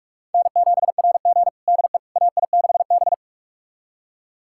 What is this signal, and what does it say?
Morse code, heard as a keyed tone, about 46 wpm, 700 Hz. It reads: N7RG DE AI6B
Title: USA Callsign DE Callsign - 4 Characters - 45wpm